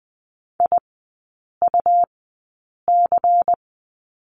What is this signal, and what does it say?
Morse code, heard as a keyed tone, about 20 wpm, spaced as for 8 wpm, 700 Hz. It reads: IUC
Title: 107 Lesson 18e - Letter m - Three Characters 20wpm